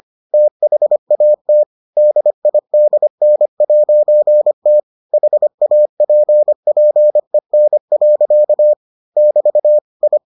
Morse code, heard as a keyed tone, about 25 wpm, 600 Hz. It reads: THAT DIDN'T HAPPEN. <BT> I